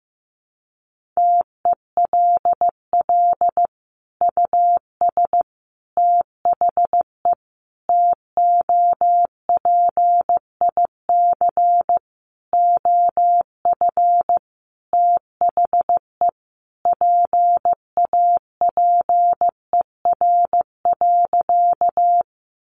Morse code, heard as a keyed tone, about 15 wpm, 700 Hz. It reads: TELL US THE TOPIC OF THE PAPER.
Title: Sentences from Top 1200 Words 15wpm